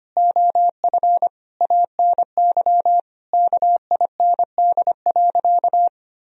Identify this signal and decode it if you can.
Morse code, heard as a keyed tone, about 25 wpm, 700 Hz. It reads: OF ANY KIND.
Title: Sentences from Top 1000 Words - Review Part 2 - 25wpm